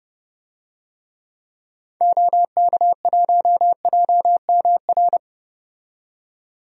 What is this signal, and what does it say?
Morse code, heard as a keyed tone, about 30 wpm, 700 Hz. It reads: OK1JMR
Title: DXCC-100 Call Signs - 6 Characters - 30wpm